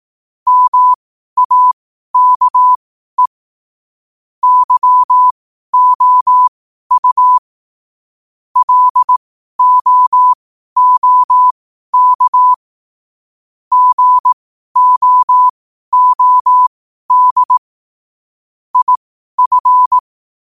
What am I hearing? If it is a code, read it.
Morse code, heard as a keyed tone, about 18 wpm, spaced as for 12 wpm, 1000 Hz. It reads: MAKE YOU LOOK GOOD IF
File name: 12wpm